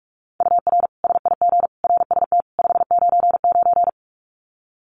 Morse code, heard as a keyed tone, about 44 wpm, 700 Hz. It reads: UR SIG RST 599
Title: Common QSO Phrases 45wpm